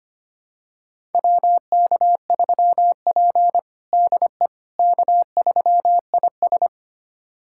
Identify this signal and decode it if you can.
Morse code, heard as a keyed tone, about 25 wpm, 700 Hz. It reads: WK3P DE K3IS
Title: USA Callsign DE Callsign - 4 Characters - 25wpm